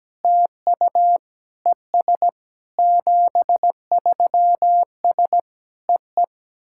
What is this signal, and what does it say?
Morse code, heard as a keyed tone, about 17 wpm, 700 Hz. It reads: TU ES 73S EE
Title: SOTA QSO Practice 17wpm